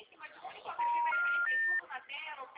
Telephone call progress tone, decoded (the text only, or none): special information tone